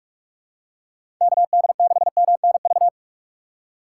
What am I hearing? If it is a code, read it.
Morse code, heard as a keyed tone, about 44 wpm, 700 Hz. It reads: KD6MNV
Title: USA Call Signs - 6 Characters - 4x Character Spacing - 45wpm